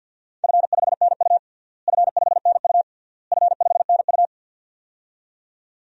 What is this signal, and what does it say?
Morse code, heard as a keyed tone, about 50 wpm, 700 Hz. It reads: F5NU F5NU F5NU
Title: DXCC-100 Call Signs - 4 Characters - Repeated 3x - 50wpm